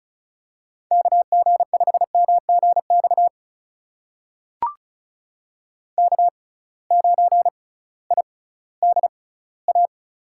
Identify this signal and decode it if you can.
Morse code, heard as a keyed tone, about 34 wpm, spaced as for 19 wpm, 700 Hz. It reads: KG5MGX K9IDA
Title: USA Call Signs - 6x Character Spacing - 35wpm